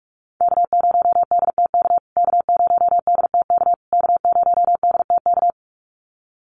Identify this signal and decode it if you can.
Morse code, heard as a keyed tone, about 44 wpm, 700 Hz. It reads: K0DTK K0DTK K0DTK
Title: USA Call Signs Repeated 3x 45wpm